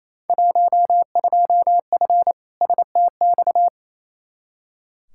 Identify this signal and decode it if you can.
Morse code, heard as a keyed tone, about 28 wpm, 700 Hz. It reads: 12F STX
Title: Field Day Exchanges Encore - 28wpm